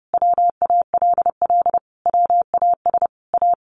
Morse code, heard as a keyed tone, about 30 wpm, 700 Hz. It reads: WALL WAS A
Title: Sentences from Top 1100 Words 30wpm